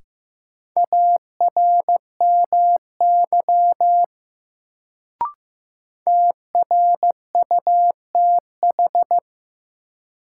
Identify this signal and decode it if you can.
Morse code, heard as a keyed tone, about 15 wpm, 700 Hz. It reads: ARMY TRUTH
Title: Sentences from Top 800 Words - New Word Drill - 15wpm